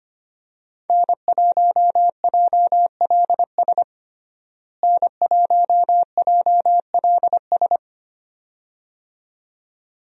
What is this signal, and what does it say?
Morse code, heard as a keyed tone, about 25 wpm, 700 Hz. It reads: N1JLS N1JLS
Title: USA Call Signs - 5 Characters - Repeated 2x - 25wpm